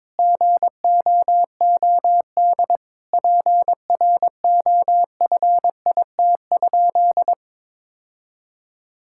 Morse code, heard as a keyed tone, about 22 wpm, 700 Hz. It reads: GOOD PROFIT?
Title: Sentences from Top 1000 Words - Review Part 5 - 22wpm